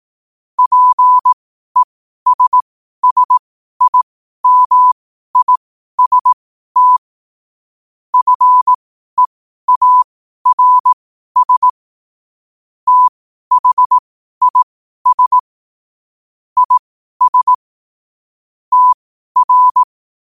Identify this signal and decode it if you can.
Morse code, heard as a keyed tone, about 18 wpm, spaced as for 12 wpm, 1000 Hz. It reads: PESSIMIST FEARS THIS IS TR
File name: qotdfiles / 12wpm